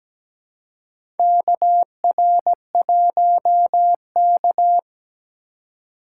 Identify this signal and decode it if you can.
Morse code, heard as a keyed tone, about 17 wpm, 700 Hz. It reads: KR1K
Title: USA Call Signs - 4 Characters - 4x Character Spacing - 17wpm